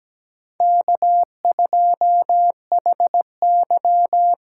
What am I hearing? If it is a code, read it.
Morse code, heard as a keyed tone, about 17 wpm, 700 Hz. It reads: K2HY
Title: USA Call Signs - 5 Characters - 8x Character Spacing - 17wpm